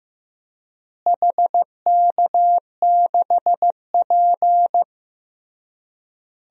Morse code, heard as a keyed tone, about 15 wpm, 700 Hz. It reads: HK6P
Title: DXCC-100 Call Signs - 4 Characters - Repeated 3x - 15wpm